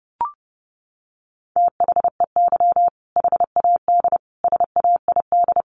Morse code, heard as a keyed tone, about 30 wpm, 700 Hz. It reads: THEY HAD SAID
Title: Sentences from Top 400 Words 30wpm